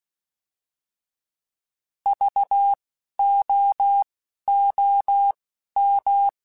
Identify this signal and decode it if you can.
Morse code, heard as a keyed tone, about 16 wpm, spaced as for 11 wpm, 800 Hz. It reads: VOOM
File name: Alphabetical / Alphabetical mp3s / Vs @10